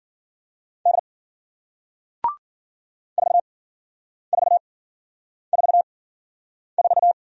Morse code, heard as a keyed tone, about 54 wpm, spaced as for 9 wpm, 700 Hz. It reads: D 4444